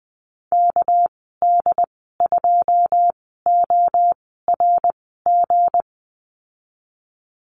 Morse code, heard as a keyed tone, about 19 wpm, spaced as for 14 wpm, 700 Hz. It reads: KD2ORG
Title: 219 Lesson 35g - Number 0 - Call Signs 20wpm